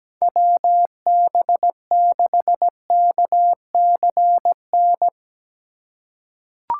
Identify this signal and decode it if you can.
Morse code, heard as a keyed tone, about 17 wpm, 700 Hz. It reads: WB6KCN
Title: USA Call Signs - 6 Characters - 2x Character Spacing - 17wpm